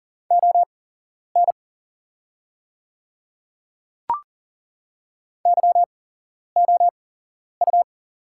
Morse code, heard as a keyed tone, about 40 wpm, spaced as for 11 wpm, 700 Hz. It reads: ON YOU
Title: 8x Character Spacing - Top 100 Words - 40wpm